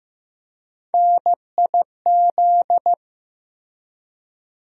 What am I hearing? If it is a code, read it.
Morse code, heard as a keyed tone, about 15 wpm, 700 Hz. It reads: NIZ
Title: USA Call Sign Postfixes - 3 Letters 15wpm